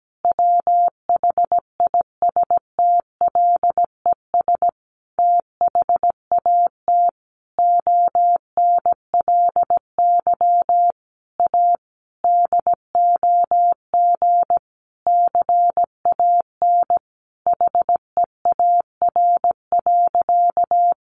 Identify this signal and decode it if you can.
Morse code, heard as a keyed tone, about 17 wpm, 700 Hz. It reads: WHISTLES THAT ONLY A DOG CAN HEAR.
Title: A Sound Of Thunder 17wpm